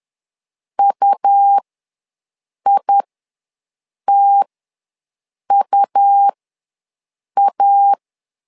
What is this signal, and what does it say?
Morse code, heard as a keyed tone, about 11 wpm, spaced as for 6 wpm, 800 Hz. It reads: UITUA
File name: LEZIONI / 2 UHDN / 2 U ws 10wpm 2017